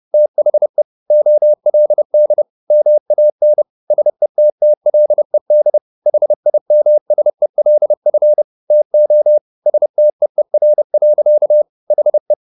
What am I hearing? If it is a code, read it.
Morse code, heard as a keyed tone, about 30 wpm, 600 Hz. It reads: THE OLD MAN SETTLED HIMSELF TO STEER. HE